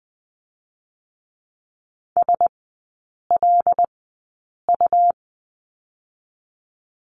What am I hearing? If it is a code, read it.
Morse code, heard as a keyed tone, about 20 wpm, 700 Hz. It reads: S L U
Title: 107 Lesson 18e - Letter m - Three Characters 20wpm